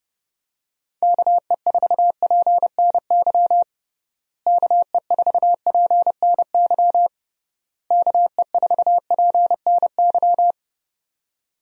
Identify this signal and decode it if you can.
Morse code, heard as a keyed tone, about 30 wpm, 700 Hz. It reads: KE4PNY KE4PNY KE4PNY
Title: USA Call Signs - 6 Characters - Repeated 3x - 30wpm